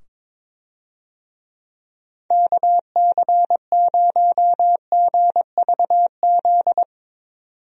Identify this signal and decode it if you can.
Morse code, heard as a keyed tone, about 22 wpm, 700 Hz. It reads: KC0GVZ
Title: USA Call Signs - 6 Characters - 22wpm